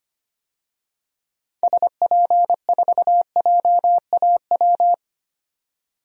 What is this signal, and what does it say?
Morse code, heard as a keyed tone, about 25 wpm, 700 Hz. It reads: SP4JAW
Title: DXCC-100 Call Signs - 6 Characters - 25wpm